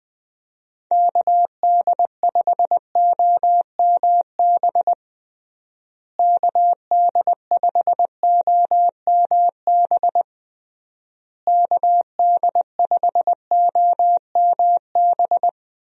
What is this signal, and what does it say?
Morse code, heard as a keyed tone, about 21 wpm, 700 Hz. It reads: KD5OMB KD5OMB KD5OMB
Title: USA Call Signs - 6 Characters - Repeated 3x - 20wpm